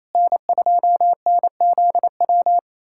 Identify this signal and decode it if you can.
Morse code, heard as a keyed tone, about 28 wpm, 700 Hz. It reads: N2NZW
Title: USA Callsign DE Callsign - 5 Characters - 28wpm